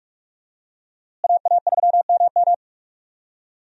Morse code, heard as a keyed tone, about 46 wpm, 700 Hz. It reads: AA3MM
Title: USA Call Signs - 5 Characters - Repeated 3x - 45wpm